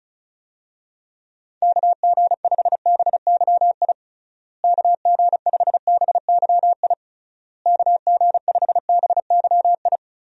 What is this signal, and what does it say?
Morse code, heard as a keyed tone, about 36 wpm, 700 Hz. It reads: KG5BYI KG5BYI KG5BYI
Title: USA Call Signs - 6 Characters - Repeated 3x - 35wpm